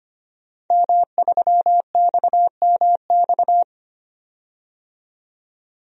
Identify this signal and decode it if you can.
Morse code, heard as a keyed tone, about 25 wpm, 700 Hz. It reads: M3XMX
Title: DXCC-100 Call Signs - 5 Characters - 25wpm